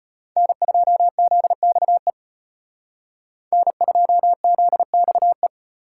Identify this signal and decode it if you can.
Morse code, heard as a keyed tone, about 32 wpm, 700 Hz. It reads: N2ZXE N2ZXE